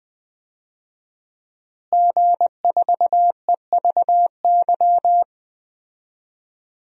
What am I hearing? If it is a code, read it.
Morse code, heard as a keyed tone, about 20 wpm, 700 Hz. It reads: G4EVY
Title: DXCC-100 Call Signs - 5 Characters - Repeated 3x - 20wpm